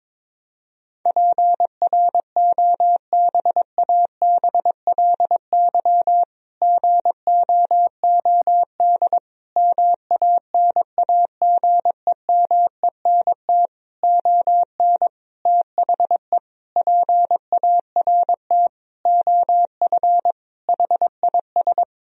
Morse code, heard as a keyed tone, about 22 wpm, 700 Hz. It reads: PROBABLY GOOD MANAGEMENT ON THE PART OF HIS
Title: Sentences from Top 1000 Words - Review Part 4 - 22wpm